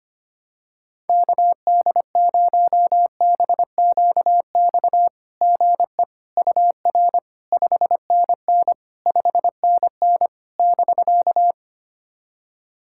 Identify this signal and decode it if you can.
Morse code, heard as a keyed tone, about 25 wpm, 700 Hz. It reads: KD0BQX GE UR 5NN 5NN <BK>
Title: SOTA QSO Practice 25wpm